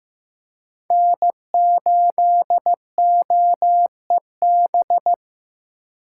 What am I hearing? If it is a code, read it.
Morse code, heard as a keyed tone, about 15 wpm, 700 Hz. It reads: N8OEB
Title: USA Call Signs - 5 Characters - 2x Character Spacing - 15wpm